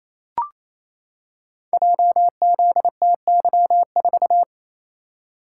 Morse code, heard as a keyed tone, about 29 wpm, 700 Hz. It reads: JZTY4
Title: Code Groups - 5 Letter:Numbers - 28wpm